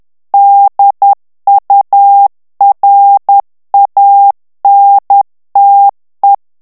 Morse code, heard as a keyed tone, about 10 wpm, 800 Hz. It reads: DURANTE
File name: morse mp3 / ITALIEN texte clair 1 vitesse 10 mots minute